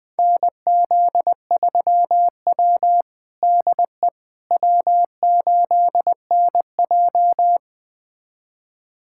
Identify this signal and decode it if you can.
Morse code, heard as a keyed tone, about 21 wpm, 700 Hz. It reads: NZ3W DE W8NJ
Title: USA Callsign DE Callsign - 4 Characters - 20wpm